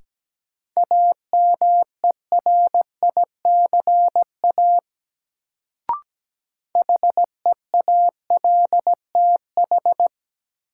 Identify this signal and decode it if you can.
Morse code, heard as a keyed tone, about 17 wpm, 700 Hz. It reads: AMERICA HEALTH
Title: Sets of 5 Words - New Word Drill - 17wpm